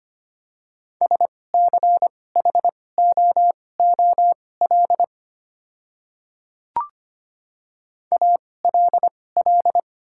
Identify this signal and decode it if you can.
Morse code, heard as a keyed tone, about 25 wpm, spaced as for 18 wpm, 700 Hz. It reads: SCHOOL ALL
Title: 2x Character Spacing - Top 300 Words - 25wpm